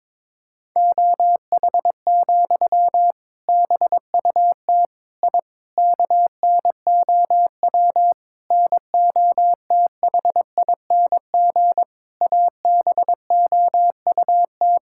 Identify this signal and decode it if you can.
Morse code, heard as a keyed tone, about 22 wpm, 700 Hz. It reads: OH, BUT I KNOW NOTHING ABOUT
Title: Sentences from Top 1000 Words - Review Part 6 - 22wpm